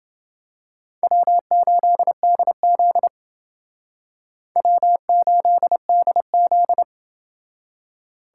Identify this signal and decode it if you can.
Morse code, heard as a keyed tone, about 29 wpm, 700 Hz. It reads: W8DZ W8DZ